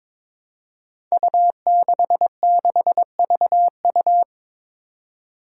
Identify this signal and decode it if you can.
Morse code, heard as a keyed tone, about 22 wpm, 700 Hz. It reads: U66VU
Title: Five Challenging Letter-Number Combinations 22wpm